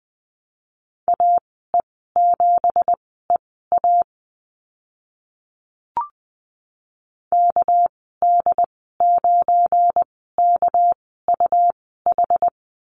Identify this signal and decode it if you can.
Morse code, heard as a keyed tone, about 20 wpm, spaced as for 14 wpm, 700 Hz. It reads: AE7EA KD9KUH
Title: 213 Lesson 34g - Number 8 - Call Signs 20wpm